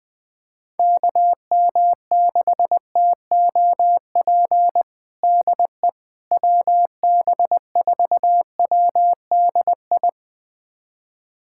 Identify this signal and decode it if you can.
Morse code, heard as a keyed tone, about 20 wpm, 700 Hz. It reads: KM6TOP DE WB4WDI
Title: USA Callsign DE Callsign - 6 Characters - 20wpm